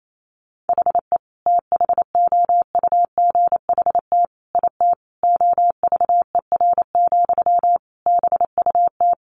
Morse code, heard as a keyed tone, about 28 wpm, 700 Hz. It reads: HE THOUGHT IT OVER, BUT
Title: Sentences from Top 1200 Words 28wpm